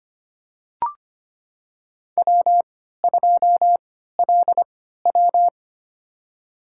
Morse code, heard as a keyed tone, about 25 wpm, spaced as for 14 wpm, 700 Hz. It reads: W2LW
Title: USA Call Signs - 4 Characters - 3x Character Spacing - 25wpm